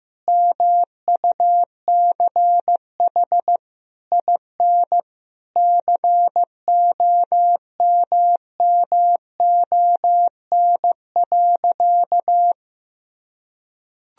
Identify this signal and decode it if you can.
Morse code, heard as a keyed tone, about 15 wpm, 700 Hz. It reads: MUCH IN COMMON.
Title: Sentences from Top 1000 Words - Review Part 3 - 15wpm